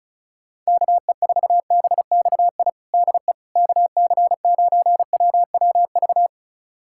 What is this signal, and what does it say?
Morse code, heard as a keyed tone, about 34 wpm, 700 Hz. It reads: KE4BXI DE KC9WWV
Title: USA Callsign DE Callsign 35wpm